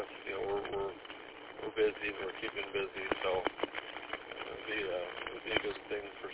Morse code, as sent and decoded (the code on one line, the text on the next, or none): none